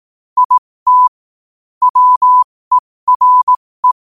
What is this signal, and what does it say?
Morse code, heard as a keyed tone, about 18 wpm, spaced as for 15 wpm, 1000 Hz. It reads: IT WERE
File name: qotdfiles / 15wpm